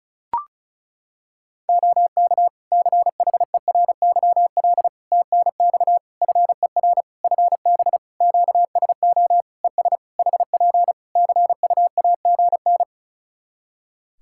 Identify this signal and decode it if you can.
Morse code, heard as a keyed tone, about 36 wpm, 700 Hz. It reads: OK CHERYL TNX FER FB QSO ES HP CUAGN
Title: Practice QSO - Exchange 4 - Ending - 35wpm